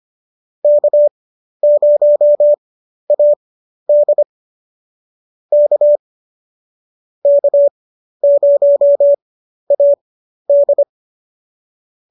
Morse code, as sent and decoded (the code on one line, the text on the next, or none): -.- ----- .- -.. / -.- / -.- ----- .- -..
K0AD K K0AD